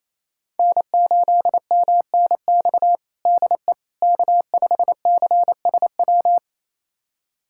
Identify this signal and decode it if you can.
Morse code, heard as a keyed tone, about 28 wpm, 700 Hz. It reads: N8MNX DE K5CSW
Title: USA Callsign DE Callsign 28wpm